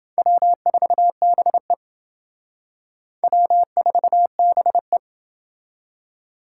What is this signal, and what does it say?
Morse code, heard as a keyed tone, about 29 wpm, 700 Hz. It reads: W4BE W4BE